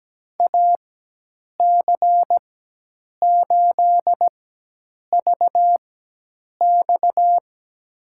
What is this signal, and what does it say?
Morse code, heard as a keyed tone, about 17 wpm, spaced as for 8 wpm, 700 Hz. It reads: AC8VX
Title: USA Call Signs - 5 Characters - 4x Character Spacing - 17wpm